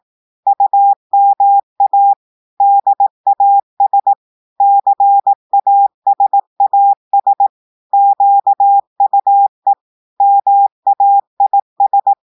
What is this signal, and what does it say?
Morse code, heard as a keyed tone, about 18 wpm, 800 Hz. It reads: UMA DAS CASAS QUE MAIS
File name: audios / portugais / portugais texte en clair 1 vitesse 18 mots minutes